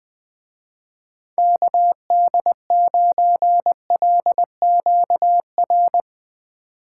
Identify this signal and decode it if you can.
Morse code, heard as a keyed tone, about 20 wpm, 700 Hz. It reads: KD9LQR
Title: Speed Racing - USA 2x3 Call Signs 20wpm